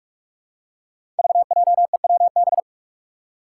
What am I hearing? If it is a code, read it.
Morse code, heard as a keyed tone, about 46 wpm, 700 Hz. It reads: VJEWB
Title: Five Challenging Letter-Number Combinations 45wpm